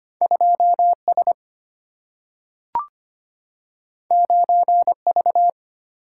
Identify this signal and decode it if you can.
Morse code, heard as a keyed tone, about 25 wpm, 700 Hz. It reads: 2S 9V
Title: Single-Letter Single-Number 25wpm